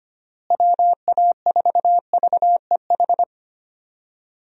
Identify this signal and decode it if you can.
Morse code, heard as a keyed tone, about 25 wpm, 700 Hz. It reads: WA4VEH
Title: USA Call Signs - 6 Characters - Repeated 2x - 25wpm